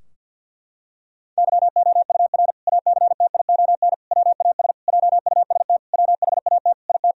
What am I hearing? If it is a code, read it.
Morse code, heard as a keyed tone, about 51 wpm, 700 Hz. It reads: YOUR ACTION WAS JUST WHAT IT
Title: Sentences from Top 1000 Words - Review Part 4 - 50wpm